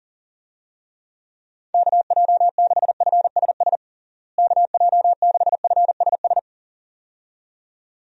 Morse code, heard as a keyed tone, about 40 wpm, 700 Hz. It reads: KJ6FSS KJ6FSS
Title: USA Call Signs - 6 Characters - Repeated 2x - 40wpm